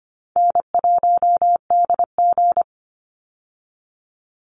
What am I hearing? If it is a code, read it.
Morse code, heard as a keyed tone, about 25 wpm, 700 Hz. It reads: N1DG
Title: Call Signs Repeated 2x - 25wpm